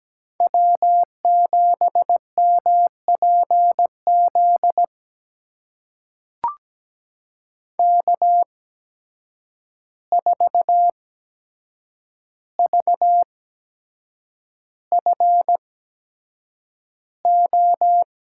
Morse code, heard as a keyed tone, about 17 wpm, spaced as for 7 wpm, 700 Hz. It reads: W7MPZ K4VFO
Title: USA Call Signs - 5 Characters - 8x Character Spacing - 17wpm